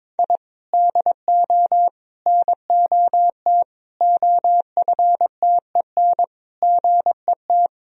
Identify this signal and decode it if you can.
Morse code, heard as a keyed tone, about 22 wpm, 700 Hz. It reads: I DO NOT OFTEN GET
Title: Sentences from Top 1000 Words - Review Part 1 - 22wpm